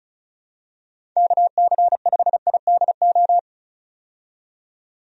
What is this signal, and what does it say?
Morse code, heard as a keyed tone, about 36 wpm, 700 Hz. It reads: KC5IDO
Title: USA Call Signs - 6 Characters - 35wpm